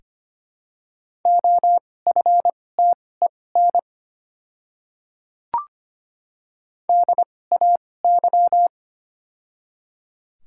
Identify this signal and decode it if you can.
Morse code, heard as a keyed tone, about 25 wpm, spaced as for 18 wpm, 700 Hz. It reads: OFTEN DAY
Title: 2x Character Spacing - Top 300 Words - 25wpm